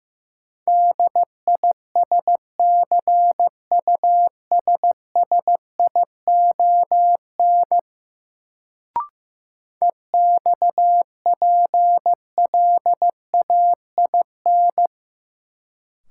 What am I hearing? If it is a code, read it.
Morse code, heard as a keyed tone, about 16 wpm, 700 Hz. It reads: DISCUSSION EXPLAIN
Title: Sentences from Top 1100 Words - New Word Drill - 15wpm